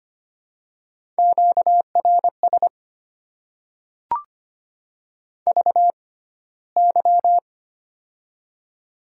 Morse code, heard as a keyed tone, about 25 wpm, spaced as for 15 wpm, 700 Hz. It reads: QRS VY
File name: Top 90 QSO Elements - 6x Character Spacing - 25wpm